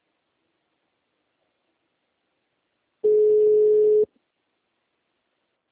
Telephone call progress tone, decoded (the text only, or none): ringback tone